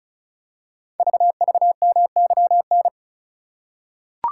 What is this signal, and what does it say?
Morse code, heard as a keyed tone, about 36 wpm, 690 Hz. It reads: VVMYN
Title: Code Groups - 5 Letters - 35wpm